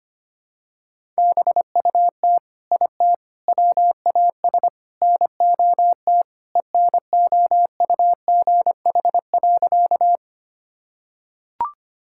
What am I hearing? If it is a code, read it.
Morse code, heard as a keyed tone, about 25 wpm, 700 Hz. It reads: BUT IT WAS NOT ENOUGH.
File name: practice / 25wpm / Sentences with 4-Word Phrases - 25wpm